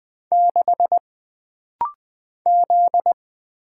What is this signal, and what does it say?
Morse code, heard as a keyed tone, about 20 wpm, 700 Hz. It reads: 6 Z
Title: Single Letter-Number - Rapid-Fire 20wpm